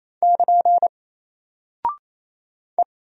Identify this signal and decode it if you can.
Morse code, heard as a keyed tone, about 28 wpm, 700 Hz. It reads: <KN> E
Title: Single Character (Letter:Number:Punctuation:Prosign) Alt 28wpm